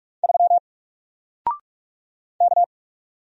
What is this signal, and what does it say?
Morse code, heard as a keyed tone, about 47 wpm, 700 Hz. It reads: 3 K
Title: Single Character (Letter:Number:Punctuation:Prosign) Alt 45wpm